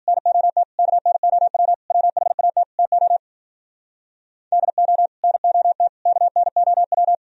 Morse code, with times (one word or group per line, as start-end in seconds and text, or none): 0.1-0.6 NOT
0.8-1.7 KNOW
1.9-2.6 WHAT
2.8-3.2 TO
4.5-5.1 DO
5.2-5.9 NOT
6.1-7.1 KNOW